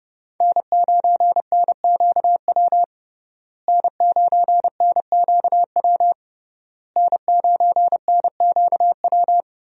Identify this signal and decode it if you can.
Morse code, heard as a keyed tone, about 30 wpm, 700 Hz. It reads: N9NQW N9NQW N9NQW
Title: USA Call Signs - 5 Characters - Repeated 3x - 30wpm